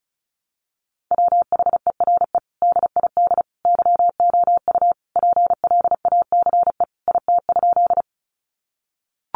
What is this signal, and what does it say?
Morse code, heard as a keyed tone, about 34 wpm, 700 Hz. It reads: WHERE DID YOU PLACE IT?
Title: Sentences from Top 200 Words 35wpm